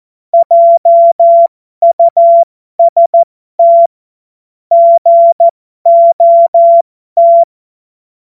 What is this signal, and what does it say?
Morse code, heard as a keyed tone, about 14 wpm, spaced as for 12 wpm, 680 Hz. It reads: JUST GOT
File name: Morsekurs / Witze